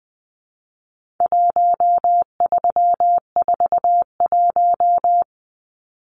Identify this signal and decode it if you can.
Morse code, heard as a keyed tone, about 20 wpm, 700 Hz. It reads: 1341
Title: Numbers - 4-Digit 20wpm